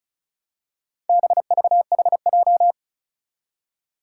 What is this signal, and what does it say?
Morse code, heard as a keyed tone, about 36 wpm, 700 Hz. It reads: BVHJ
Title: Four Challenging Letter-Number Combinations 35wpm